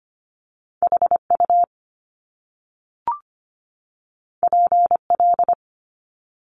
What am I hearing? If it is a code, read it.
Morse code, heard as a keyed tone, about 25 wpm, 700 Hz. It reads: H U P L
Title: Top Two Letter Combinations in Words 25wpm